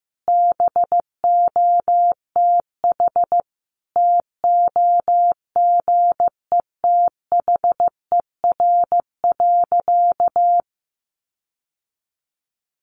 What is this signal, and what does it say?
Morse code, heard as a keyed tone, about 15 wpm, 700 Hz. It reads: BOTH TOGETHER.
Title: Sentences from Top 1300 Words 15wpm